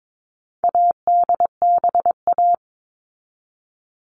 Morse code, heard as a keyed tone, about 22 wpm, 700 Hz. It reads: ADBA
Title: All Cut Numbers - 4-Digit 22wpm